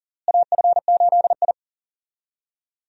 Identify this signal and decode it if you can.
Morse code, heard as a keyed tone, about 40 wpm, 700 Hz. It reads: AF8I
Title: USA Call Signs - 4 Characters - Repeated 3x - 40wpm